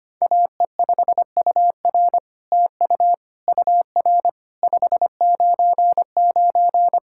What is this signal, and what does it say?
Morse code, heard as a keyed tone, about 25 wpm, 700 Hz. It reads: AE5UR TU UR 599
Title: SOTA QSO Practice 25wpm